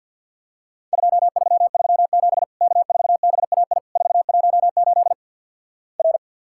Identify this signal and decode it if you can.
Morse code, heard as a keyed tone, about 54 wpm, 700 Hz. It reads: 2337 K4BAI 418 R
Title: Worked All Europe DX Contest - QTC Exchange with Repeats - 50wpm